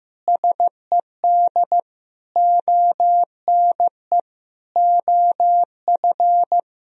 Morse code, heard as a keyed tone, about 15 wpm, 700 Hz. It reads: SED ONE OF
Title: Sentences from Top 300 Words and Abbreviations 15wpm